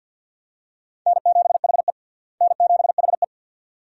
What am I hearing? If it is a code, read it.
Morse code, heard as a keyed tone, about 51 wpm, 700 Hz. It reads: N7HE N7HE
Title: USA Call Signs - 4 Characters - Repeated 2x - 50wpm